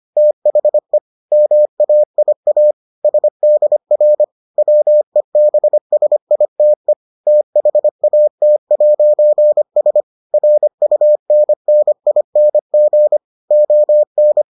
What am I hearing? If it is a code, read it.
Morse code, heard as a keyed tone, about 25 wpm, 600 Hz. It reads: THE MAIA SDR WEBSITE THAT'S RUNNING ON